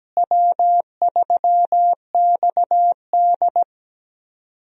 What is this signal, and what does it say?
Morse code, heard as a keyed tone, about 17 wpm, 700 Hz. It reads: W3XD
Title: USA Call Signs - 4 Characters - Repeated 3x - 17wpm